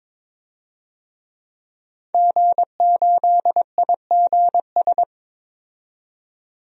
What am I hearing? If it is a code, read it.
Morse code, heard as a keyed tone, about 22 wpm, 700 Hz. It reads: G8IGS
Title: DXCC-100 Call Signs - 5 Characters - 22wpm